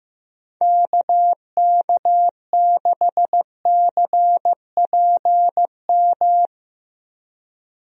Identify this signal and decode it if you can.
Morse code, heard as a keyed tone, about 15 wpm, 700 Hz. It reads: KK6CPM
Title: USA Call Signs - 6 Characters - 15wpm